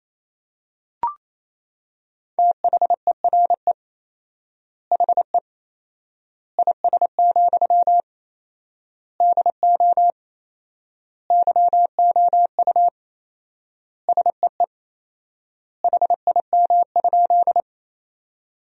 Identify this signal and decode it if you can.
Morse code, heard as a keyed tone, about 28 wpm, 700 Hz. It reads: THERE HE IS, DO YOU SEE HIM?
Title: 4x Word-Spacing - Sentences from Top 100 Words - 28wpm